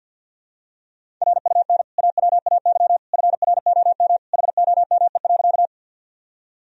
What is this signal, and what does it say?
Morse code, heard as a keyed tone, about 52 wpm, 700 Hz. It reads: RUN AWAY FROM HOME.
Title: Sentences from Top 1000 Words - Review Part 1 - 50wpm